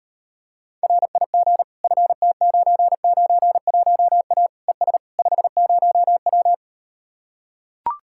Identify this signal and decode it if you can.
Morse code, heard as a keyed tone, about 38 wpm, 700 Hz. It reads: RIG FT991A ES 50W